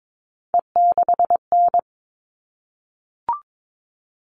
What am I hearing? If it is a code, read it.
Morse code, heard as a keyed tone, about 22 wpm, 700 Hz. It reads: E6N
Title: All Cut Numbers - 3-Digit 22wpm